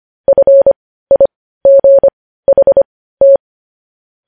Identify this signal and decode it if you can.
Morse code, heard as a keyed tone, about 25 wpm, spaced as for 15 wpm, 550 Hz. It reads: FIGHT
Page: News Headlines in Morse Code at 15 WPM